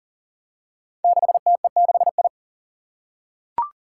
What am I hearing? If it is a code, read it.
Morse code, heard as a keyed tone, about 40 wpm, 700 Hz. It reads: 6TE6I
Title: Five Challenging Letter-Number Combinations 40wpm